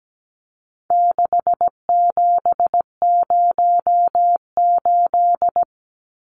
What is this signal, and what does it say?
Morse code, heard as a keyed tone, about 17 wpm, 700 Hz. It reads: 6708
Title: Numbers - 4-Digit 17wpm